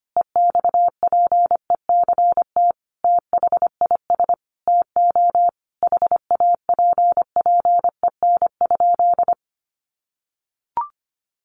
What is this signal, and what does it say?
Morse code, heard as a keyed tone, about 25 wpm, 700 Hz. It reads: EXPECT THIS TO HAPPEN?
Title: Sentences from Top 1100 Words 25wpm